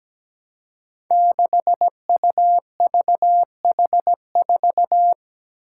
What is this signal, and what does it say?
Morse code, heard as a keyed tone, about 17 wpm, 700 Hz. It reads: 6UVH4
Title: Five Challenging Letter-Number Combinations 17wpm